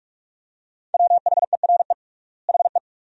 Morse code, heard as a keyed tone, about 43 wpm, 700 Hz. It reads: WHERE HE
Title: 3x Word-Spacing - Sentences from Top 400 Words - 45wpm